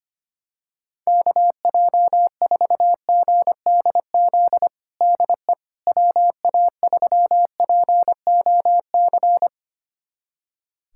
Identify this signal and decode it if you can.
Morse code, heard as a keyed tone, about 25 wpm, 700 Hz. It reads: KJ4GDZ DE WA3POC
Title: USA Callsign DE Callsign 25wpm